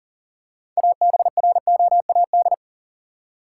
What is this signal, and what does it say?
Morse code, heard as a keyed tone, about 40 wpm, 700 Hz. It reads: ABROAD
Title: Six Letter Words - 40wpm